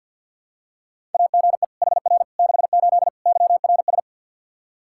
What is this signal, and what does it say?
Morse code, heard as a keyed tone, about 52 wpm, 700 Hz. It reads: AGE HR 68 YRS
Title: K1USN SST QSO Practice 50wpm